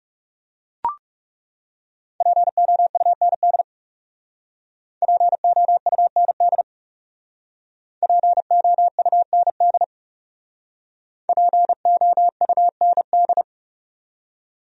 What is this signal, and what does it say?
Morse code, heard as a keyed tone, about 36 wpm, 700 Hz. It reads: POUND POUND POUND POUND
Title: Speed-Racing Top 500 Words 30wpm